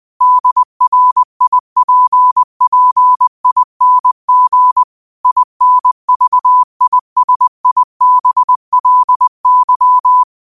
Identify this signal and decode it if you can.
Morse code, heard as a keyed tone, about 20 wpm, 1000 Hz. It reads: DRIPPING INVISIBLY
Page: New quotes every day in morse code at 20 Words per minute.